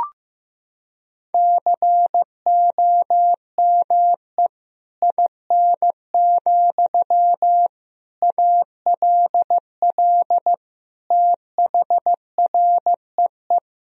Morse code, heard as a keyed tone, about 15 wpm, 700 Hz. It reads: COME IN, ALL THREE
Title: Sentences from Top 1000 Words - Review Part 1 - 15wpm